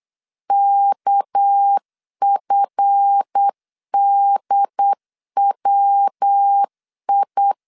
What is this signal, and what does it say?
Morse code, heard as a keyed tone, about 8 wpm, 800 Hz. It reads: KFDWI
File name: LEZIONI / 8 YJ45 / 8 YJ4 8wpm 3246